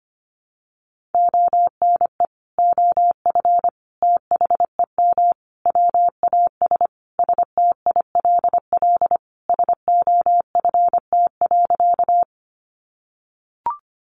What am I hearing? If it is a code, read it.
Morse code, heard as a keyed tone, about 25 wpm, 700 Hz. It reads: ONE OF THEM WAS STILL SOFT.
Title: Sentences from Top 1500 Words 25wpm